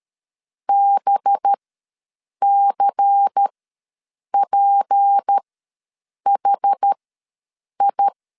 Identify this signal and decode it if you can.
Morse code, heard as a keyed tone, about 13 wpm, spaced as for 7 wpm, 800 Hz. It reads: BCPHI
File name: LEZIONI / 6 PKX0 / 6 P ws 12wpm 3117